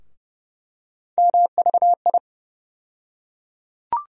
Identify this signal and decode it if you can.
Morse code, heard as a keyed tone, about 30 wpm, 700 Hz. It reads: MVI
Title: USA Call Sign Postfixes - 3 Letters 30wpm